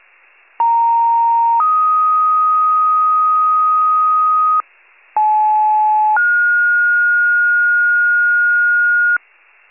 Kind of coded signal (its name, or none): two-tone page